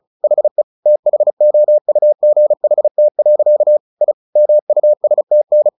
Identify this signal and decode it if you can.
Morse code, heard as a keyed tone, about 34 wpm, 600 Hz. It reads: HE THOUGHT. I MUSTN
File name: the old man and the sea 2 35wpm